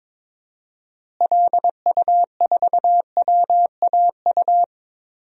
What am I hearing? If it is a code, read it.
Morse code, heard as a keyed tone, about 22 wpm, 700 Hz. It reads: LU4WAU
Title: DXCC-100 Call Signs - 6 Characters - Repeated 3x - 22wpm